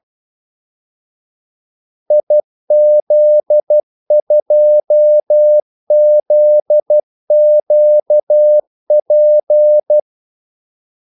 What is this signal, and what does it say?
Morse code, heard as a keyed tone, about 12 wpm, 600 Hz. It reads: IZ2ZQP
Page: 2 ekstra ordmellemrum mellem kaldesignalerne